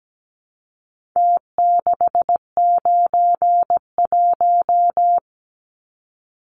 Morse code, heard as a keyed tone, about 17 wpm, 700 Hz. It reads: T691
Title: Numbers - Contest Style - 4-Digit 17wpm